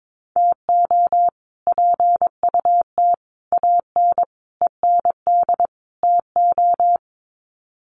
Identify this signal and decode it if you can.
Morse code, heard as a keyed tone, about 22 wpm, 700 Hz. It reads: TO PUT AN END TO
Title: Sets of 5 Words Encore2 22wpm